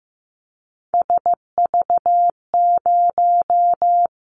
Morse code, heard as a keyed tone, about 15 wpm, 700 Hz. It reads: SV0
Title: Callsign Prefixes for Top 50 DXCC - 15wpm